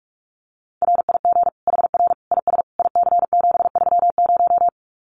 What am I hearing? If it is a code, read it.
Morse code, heard as a keyed tone, about 44 wpm, 700 Hz. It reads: RIG HR IS IC730
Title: Common QSO Phrases 45wpm